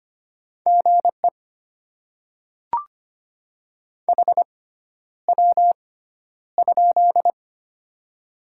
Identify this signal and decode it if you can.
Morse code, heard as a keyed tone, about 25 wpm, spaced as for 11 wpm, 700 Hz. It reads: GE HW?
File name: Top 20 QSO Elements - 6x Character Spacing - 25wpm